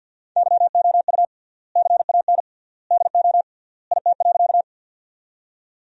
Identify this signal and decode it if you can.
Morse code, heard as a keyed tone, about 50 wpm, 700 Hz. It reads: YOU CAN DO IT.
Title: 3x Word-Spacing - Sentences from Top 100 Words - 50wpm